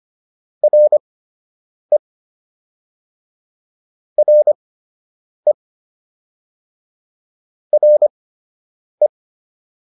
Morse code, heard as a keyed tone, about 25 wpm, spaced as for 8 wpm, 600 Hz. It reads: RE RE RE